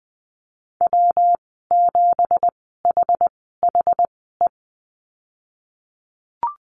Morse code, heard as a keyed tone, about 20 wpm, spaced as for 15 wpm, 700 Hz. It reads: W7HHE
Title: 219 Lesson 35g - Number 0 - Call Signs 20wpm